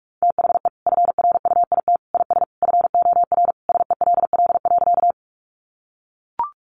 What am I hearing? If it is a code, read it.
Morse code, heard as a keyed tone, about 44 wpm, 700 Hz. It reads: THE FRUIT IS FOR SELL.
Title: Sentences from Top 1300 Words 45wpm